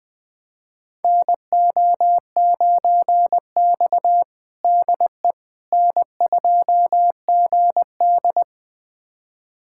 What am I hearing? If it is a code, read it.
Morse code, heard as a keyed tone, about 20 wpm, 700 Hz. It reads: NO9X DE N2GD
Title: USA Callsign DE Callsign - 4 Characters - 20wpm